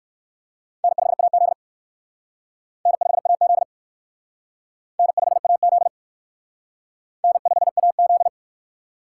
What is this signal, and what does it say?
Morse code, heard as a keyed tone, about 57 wpm, 700 Hz. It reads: N5AZ N5AZ N5AZ N5AZ